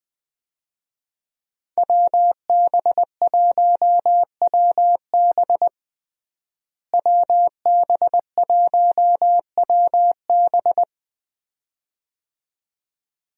Morse code, heard as a keyed tone, about 20 wpm, 700 Hz. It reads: WB1WB WB1WB
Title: USA Call Signs - 5 Characters - Repeated 2x - 20wpm